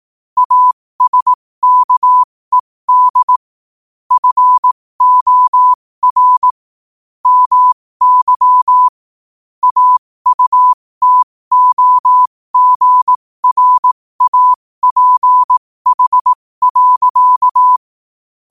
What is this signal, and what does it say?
Morse code, heard as a keyed tone, about 18 wpm, spaced as for 15 wpm, 1000 Hz. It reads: ASKED FOR MY AUTOGRAPH.
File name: 15wpm